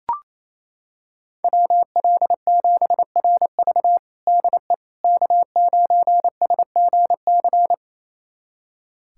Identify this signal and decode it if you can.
Morse code, heard as a keyed tone, about 28 wpm, 700 Hz. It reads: WL7RV DE K9SGC
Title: USA Callsign DE Callsign - 5 Characters - 28wpm